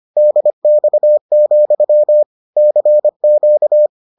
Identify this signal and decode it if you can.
Morse code, heard as a keyed tone, about 25 wpm, 600 Hz. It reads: DX, CQ